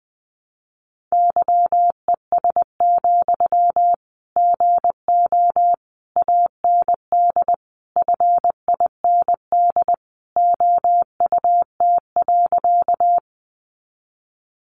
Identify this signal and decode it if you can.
Morse code, heard as a keyed tone, about 20 wpm, 700 Hz. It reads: YES, GO AND FIND OUT.
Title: Sentences from Top 500 Words 20wpm